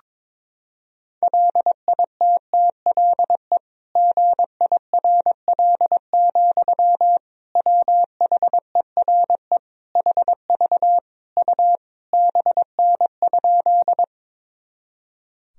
Morse code, heard as a keyed tone, about 22 wpm, 700 Hz. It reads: LITTLE GIRL, WHERE HV U BN?
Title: Sentences from Top 400 Words and Abbreviations 22wpm